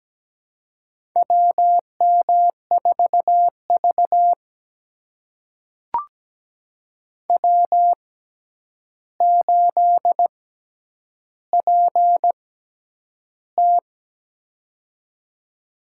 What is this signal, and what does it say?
Morse code, heard as a keyed tone, about 17 wpm, spaced as for 9 wpm, 700 Hz. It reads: WM4V W8PT
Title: USA Call Signs - 4 Characters - 6x Character Spacing - 17wpm